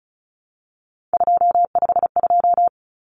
Morse code, heard as a keyed tone, about 34 wpm, 700 Hz. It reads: 252
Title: Numbers - 3-Digit 35wpm